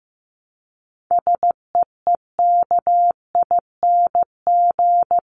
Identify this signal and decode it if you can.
Morse code, heard as a keyed tone, about 15 wpm, 700 Hz. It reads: SEEKING
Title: ING Suffix - Top 100 Words with Base Words 15wpm